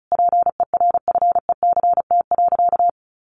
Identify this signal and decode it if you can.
Morse code, heard as a keyed tone, about 34 wpm, 700 Hz. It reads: PERFECT.
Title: Sentences from Top 1500 Words 35wpm